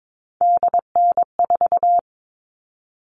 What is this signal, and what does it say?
Morse code, heard as a keyed tone, about 22 wpm, 700 Hz. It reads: DN4
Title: All Cut Numbers - 3-Digit 22wpm